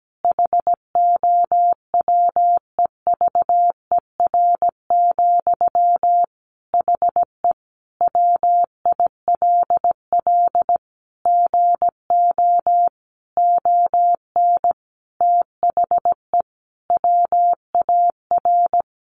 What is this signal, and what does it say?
Morse code, heard as a keyed tone, about 17 wpm, 700 Hz. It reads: HOWEVER, HE WILL GO ON THE WAR
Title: Sentences from Top 1400 Words 17wpm